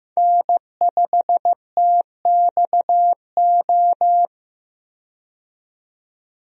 Morse code, heard as a keyed tone, about 15 wpm, 700 Hz. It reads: N5TXO
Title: USA Call Signs - 5 Characters - 15wpm